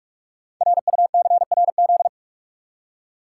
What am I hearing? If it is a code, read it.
Morse code, heard as a keyed tone, about 44 wpm, 700 Hz. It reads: RUCRZ
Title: Code Groups - 5 Letters - 45wpm